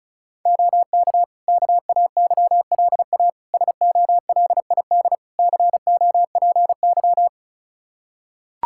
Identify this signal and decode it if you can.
Morse code, heard as a keyed tone, about 34 wpm, 700 Hz. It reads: OK KAYLA SOLID COPY
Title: K1USN SST QSO Practice 35wpm